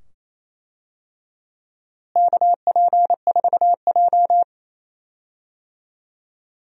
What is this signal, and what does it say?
Morse code, heard as a keyed tone, about 28 wpm, 700 Hz. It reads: KP4J
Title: DXCC-100 Call Signs - 4 Characters - Repeated 2x - 28wpm